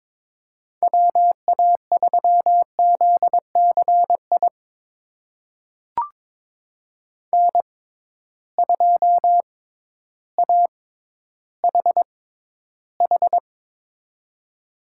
Morse code, heard as a keyed tone, about 22 wpm, spaced as for 12 wpm, 700 Hz. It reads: WA3ZCI N2AHH
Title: USA Call Signs - 6x Character Spacing - 22wpm